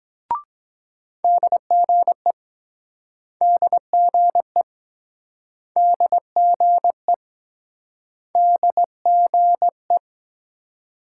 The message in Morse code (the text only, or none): DGE DGE DGE DGE